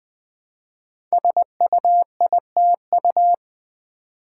Morse code, heard as a keyed tone, about 20 wpm, 700 Hz. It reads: SUITU
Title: Code Groups - 5 Letters - 20wpm